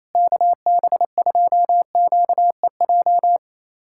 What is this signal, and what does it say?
Morse code, heard as a keyed tone, about 28 wpm, 700 Hz. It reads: KB2QEJ
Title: USA Callsign DE Callsign 28wpm